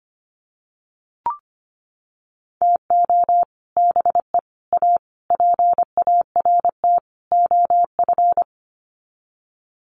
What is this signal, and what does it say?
Morse code, heard as a keyed tone, about 25 wpm, 700 Hz. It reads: TO BE A PART OF
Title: Sets of 5 Words 25wpm